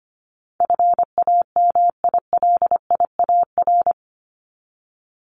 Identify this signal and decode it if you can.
Morse code, heard as a keyed tone, about 25 wpm, 700 Hz. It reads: FAMILIAR
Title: Top 751-1000 Words 25wpm